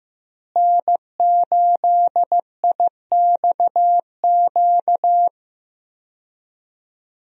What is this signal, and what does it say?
Morse code, heard as a keyed tone, about 15 wpm, 700 Hz. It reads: N8IXQ
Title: USA Call Signs - 5 Characters - Repeated 2x - 15wpm